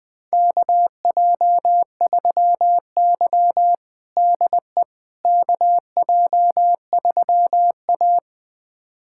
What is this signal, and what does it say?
Morse code, heard as a keyed tone, about 20 wpm, 700 Hz. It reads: KJ3Y DE KJ3A
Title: USA Callsign DE Callsign - 4 Characters - 20wpm